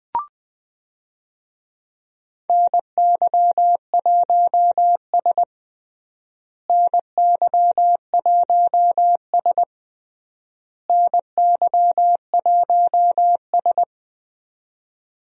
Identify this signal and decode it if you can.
Morse code, heard as a keyed tone, about 20 wpm, 700 Hz. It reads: NY1S NY1S NY1S
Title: USA Call Signs - 4 Characters - Repeated 3x - 20wpm